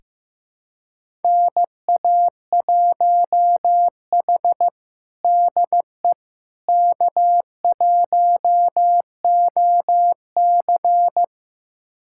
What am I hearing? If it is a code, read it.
Morse code, heard as a keyed tone, about 16 wpm, 700 Hz. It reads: NA1H DE K1OC
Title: USA Callsign DE Callsign - 4 Characters - 15wpm